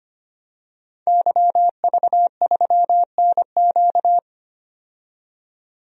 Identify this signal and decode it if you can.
Morse code, heard as a keyed tone, about 25 wpm, 700 Hz. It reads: YV3NQ
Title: DXCC-100 Call Signs - 5 Characters - 25wpm